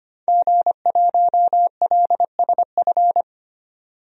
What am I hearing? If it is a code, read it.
Morse code, heard as a keyed tone, about 25 wpm, 700 Hz. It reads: G1LSF
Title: DXCC-100 Call Signs - 5 Characters - 25wpm